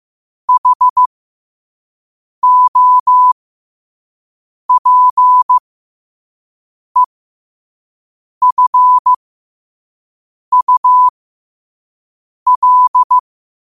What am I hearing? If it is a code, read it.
Morse code, heard as a keyed tone, about 15 wpm, spaced as for 5 wpm, 1000 Hz. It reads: HOPEFUL
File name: qotdfiles / 5wpm